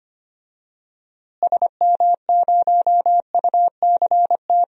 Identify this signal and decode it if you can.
Morse code, heard as a keyed tone, about 25 wpm, 700 Hz. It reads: SM0UCT
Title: DXCC-100 Call Signs - 6 Characters - Repeated 3x - 25wpm